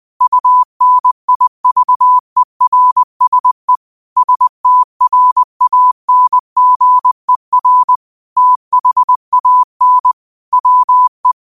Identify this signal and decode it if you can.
Morse code, heard as a keyed tone, about 20 wpm, 1000 Hz. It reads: UNIVERSE STRANGER THAN WE
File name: qotdfiles / 20wpm